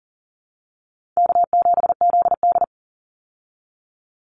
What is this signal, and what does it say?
Morse code, heard as a keyed tone, about 40 wpm, 700 Hz. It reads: K7ZD
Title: Easy Call Signs Repeated 2x - 40wpm